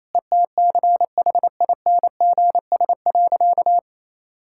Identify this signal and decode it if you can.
Morse code, heard as a keyed tone, about 28 wpm, 700 Hz. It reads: ETCHINGS.
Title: Pangram Challenge v2 - 28wpm